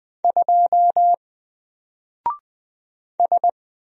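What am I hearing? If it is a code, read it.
Morse code, heard as a keyed tone, about 20 wpm, 700 Hz. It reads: 2 S
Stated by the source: Single letter and number Morse drill at 20 wpm, from N4LCD.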